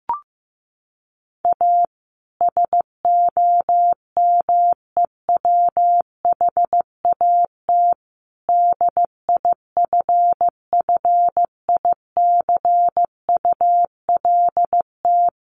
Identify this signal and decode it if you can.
Morse code, heard as a keyed tone, about 15 wpm, 700 Hz. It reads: A SOMEWHAT DIFFICULT
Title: Sentences from Top 1300 Words 15wpm